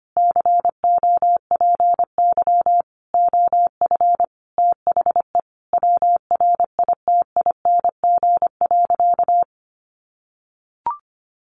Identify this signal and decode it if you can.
Morse code, heard as a keyed tone, about 25 wpm, 700 Hz. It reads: COPY OF THE WRITING.
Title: Sentences from Top 1400 Words 25wpm